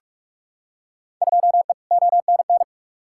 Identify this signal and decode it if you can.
Morse code, heard as a keyed tone, about 47 wpm, 700 Hz. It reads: 2E ONN
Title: Field Day Exchanges Encore - 45wpm